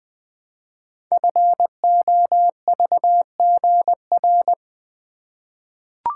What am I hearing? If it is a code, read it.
Morse code, heard as a keyed tone, about 20 wpm, 700 Hz. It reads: FOVGR
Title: Code Groups - 5 Letters - 20wpm